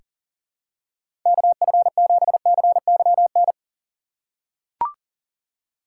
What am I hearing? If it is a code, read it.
Morse code, heard as a keyed tone, about 40 wpm, 700 Hz. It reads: KF7CYN
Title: USA Call Signs - 6 Characters - 2x Character Spacing - 40wpm